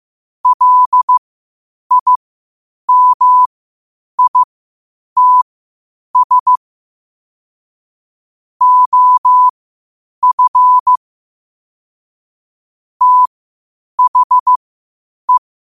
Morse code, heard as a keyed tone, about 15 wpm, spaced as for 8 wpm, 1000 Hz. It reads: LIMITS OF THE
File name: qotdfiles / 8wpm